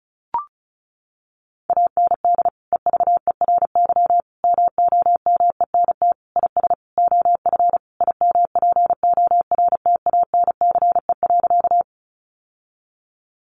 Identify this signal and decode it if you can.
Morse code, heard as a keyed tone, about 34 wpm, 700 Hz. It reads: AND EVERY MOMENT IS OF IMPORTANCE.
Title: Sentences from Top 1200 Words 35wpm